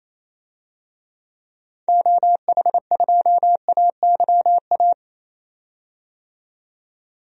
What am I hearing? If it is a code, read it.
Morse code, heard as a keyed tone, about 28 wpm, 700 Hz. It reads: OH2AYA
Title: DXCC-100 Call Signs - 6 Characters - 28wpm